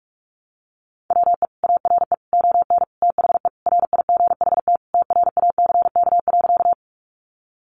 Morse code, heard as a keyed tone, about 44 wpm, 700 Hz. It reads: WE ARE ON THE RIGHT TRACK.
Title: Sentences from Top 1100 Words 45wpm